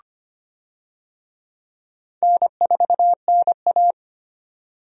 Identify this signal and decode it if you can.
Morse code, heard as a keyed tone, about 25 wpm, 700 Hz. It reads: N4NA
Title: USA Call Signs - 4 Characters - 25wpm